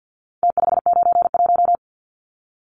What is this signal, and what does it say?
Morse code, heard as a keyed tone, about 48 wpm, 700 Hz. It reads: T591
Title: Numbers - Contest Style - 4-Digit 50wpm